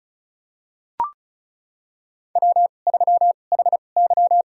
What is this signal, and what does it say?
Morse code, heard as a keyed tone, about 34 wpm, spaced as for 25 wpm, 700 Hz. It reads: W3HY
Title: USA Call Signs - 4 Characters - 2x Character Spacing - 35wpm